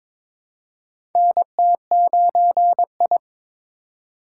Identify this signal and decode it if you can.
Morse code, heard as a keyed tone, about 22 wpm, 700 Hz. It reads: NT9I
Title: USA Call Signs - 2x Character Spacing - 22wpm